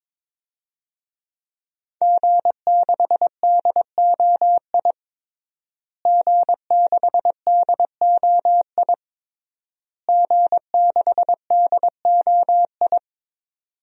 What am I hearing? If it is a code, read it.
Morse code, heard as a keyed tone, about 22 wpm, 700 Hz. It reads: G6DOI G6DOI G6DOI
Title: DXCC-100 Call Signs - 5 Characters - Repeated 3x - 22wpm